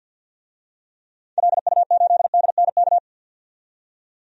Morse code, heard as a keyed tone, about 52 wpm, 700 Hz. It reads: LU8DNK
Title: DXCC-100 Call Signs - 6 Characters - 50wpm